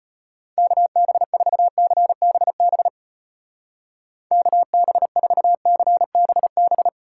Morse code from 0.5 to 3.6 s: KB4CBB

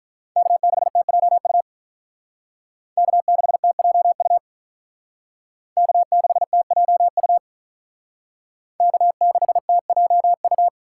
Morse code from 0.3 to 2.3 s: K6TJU

